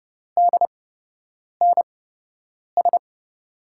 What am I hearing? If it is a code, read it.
Morse code, heard as a keyed tone, about 30 wpm, spaced as for 8 wpm, 700 Hz. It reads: DNS
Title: USA Call Signs - 8x Character Spacing - 30wpm